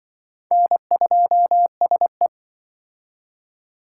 Morse code, heard as a keyed tone, about 23 wpm, 700 Hz. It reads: N2SE
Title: Speed Racing - USA 1x2 Call Signs 20wpm